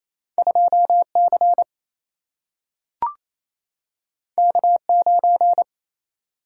Morse code, heard as a keyed tone, about 28 wpm, 700 Hz. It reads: 2C K9
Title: Single-Letter Single-Number 28wpm